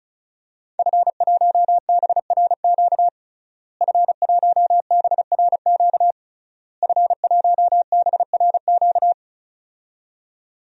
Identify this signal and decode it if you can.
Morse code, heard as a keyed tone, about 36 wpm, 700 Hz. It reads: F1BRQ F1BRQ F1BRQ
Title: DXCC-100 Call Signs - 5 Characters - Repeated 3x - 35wpm